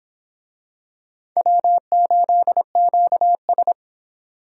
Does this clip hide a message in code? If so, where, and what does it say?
Yes: Morse code from 1.4 to 3.7 s, heard as a keyed tone, about 26 wpm, 700 Hz. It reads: W8QS